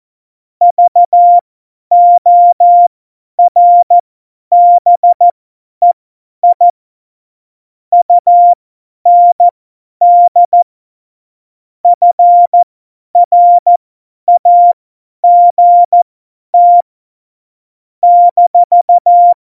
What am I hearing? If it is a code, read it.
Morse code, heard as a keyed tone, about 14 wpm, spaced as for 10 wpm, 700 Hz. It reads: VORBEI UND FRAGT -
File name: Morsekurs / Witze / Witz 03. November 2020, 50 F70